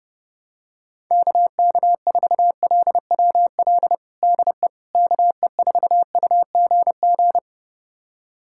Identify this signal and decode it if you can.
Morse code, heard as a keyed tone, about 30 wpm, 700 Hz. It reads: KK4LWL DE KE4UGG
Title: USA Callsign DE Callsign - 6 Characters - 30wpm